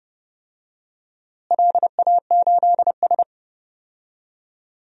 Morse code, heard as a keyed tone, about 30 wpm, 700 Hz. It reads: LA8S
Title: DXCC-100 Call Signs - 4 Characters - Repeated 2x - 30wpm